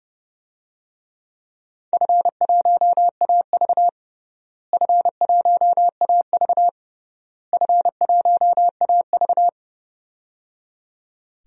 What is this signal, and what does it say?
Morse code, heard as a keyed tone, about 30 wpm, 700 Hz. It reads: F1AV F1AV F1AV
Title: DXCC-100 Call Signs - 4 Characters - Repeated 3x - 30wpm